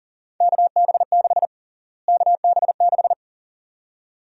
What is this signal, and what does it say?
Morse code, heard as a keyed tone, about 40 wpm, 700 Hz. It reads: KB6 KB6
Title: USA Call Sign Prefixes - Repeated 2x - 40wpm